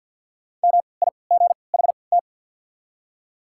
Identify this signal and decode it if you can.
Morse code, heard as a keyed tone, about 54 wpm, spaced as for 29 wpm, 700 Hz. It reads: MIGHT
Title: 3x Character Spacing - Top 300 Words - 50wpm